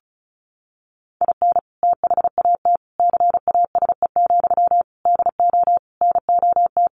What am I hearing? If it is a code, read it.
Morse code, heard as a keyed tone, about 34 wpm, 700 Hz. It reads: IN THAT CASE, DO NOT
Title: Sentences from Top 1300 Words 35wpm